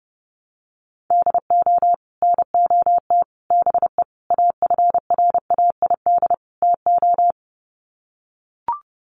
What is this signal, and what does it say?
Morse code, heard as a keyed tone, about 30 wpm, 700 Hz. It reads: DO NOT BE AFRAID TO
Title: Sets of 5 Words Encore2 30wpm